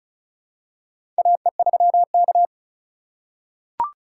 Morse code, heard as a keyed tone, about 36 wpm, 700 Hz. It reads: AE3K
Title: USA Call Signs - 4x Character Spacing - 35wpm